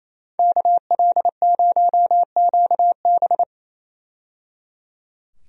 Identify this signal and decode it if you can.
Morse code, heard as a keyed tone, about 27 wpm, 700 Hz. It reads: KL0QB
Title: USA Call Signs - 5 Characters - 28wpm